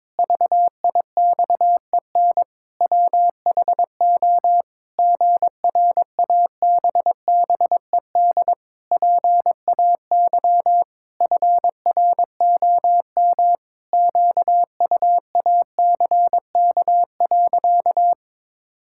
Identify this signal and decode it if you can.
Morse code, heard as a keyed tone, about 22 wpm, 700 Hz. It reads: VIXEN WHO GRABBED PAY FROM QUACK.
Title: Pangram Challenge v2 - 22wpm